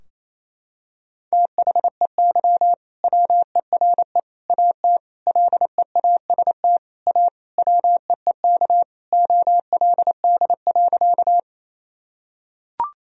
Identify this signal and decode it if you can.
Morse code, heard as a keyed tone, about 28 wpm, 700 Hz. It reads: THEY WERE AT LEAST A WEEK OLD.
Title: Sentences from Top 1000 Words - Review Part 2 - 28wpm